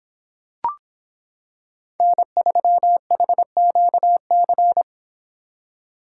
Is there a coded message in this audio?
Yes: Morse code from 2.0 to 4.8 s, heard as a keyed tone, about 26 wpm, 700 Hz. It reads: N3HQC